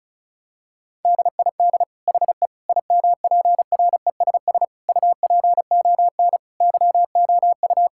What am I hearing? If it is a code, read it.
Morse code, heard as a keyed tone, about 36 wpm, 700 Hz. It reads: DID HE IMPRESS UPON YOU
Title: Sentences from Top 1600 Words 35wpm